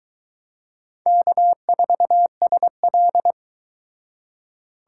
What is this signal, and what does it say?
Morse code, heard as a keyed tone, about 23 wpm, 700 Hz. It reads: K4SL